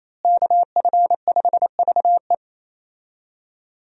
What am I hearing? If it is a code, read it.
Morse code, heard as a keyed tone, about 28 wpm, 700 Hz. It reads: KF5VE
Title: USA Call Signs - 5 Characters - Repeated 2x - 28wpm